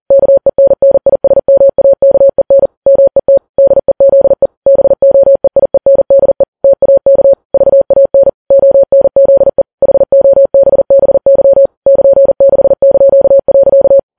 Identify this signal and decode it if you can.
Morse code, heard as a keyed tone, about 40 wpm, 550 Hz. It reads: KENNISMAKEN MET DEZE BOEIENDE TAK VAN ONZE HOBBY <KN>6).
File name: Files Web tls / Morse on9fv / morse40kps